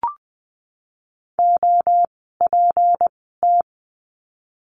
Morse code, heard as a keyed tone, about 20 wpm, spaced as for 14 wpm, 700 Hz. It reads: OPT
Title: 153 Lesson 25f - Letter p - Words 20wpm